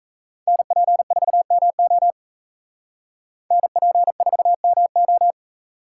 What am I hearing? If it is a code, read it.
Morse code, heard as a keyed tone, about 44 wpm, 700 Hz. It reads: NP4MO NP4MO